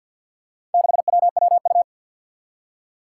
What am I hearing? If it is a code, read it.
Morse code, heard as a keyed tone, about 51 wpm, 700 Hz. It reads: 6WWU
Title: Four Challenging Letter-Number Combinations 50wpm